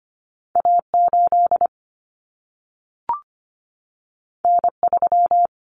Morse code, heard as a keyed tone, about 25 wpm, 700 Hz. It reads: A8 N3
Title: Common Cut Numbers - 2-Digit 25wpm